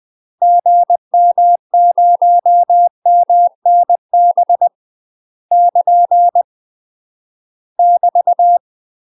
Morse code, heard as a keyed tone, about 20 wpm, 700 Hz. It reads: GM0MNB <KN> <BT>
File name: UK MASTER QSOs TRAINING / 20WPM / UK MASTER QSOs 0065